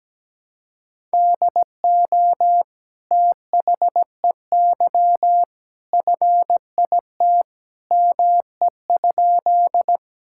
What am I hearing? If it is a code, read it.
Morse code, heard as a keyed tone, about 17 wpm, 700 Hz. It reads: DO THEY FIT ME?
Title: Sentences from Top 1000 Words - Review Part 5 - 17wpm